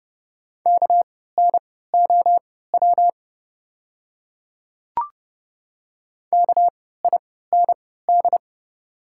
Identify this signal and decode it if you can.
Morse code, heard as a keyed tone, about 30 wpm, spaced as for 17 wpm, 690 Hz. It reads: KNOW KIND
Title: 3x Character Spacing - Top 200 Words - 30wpm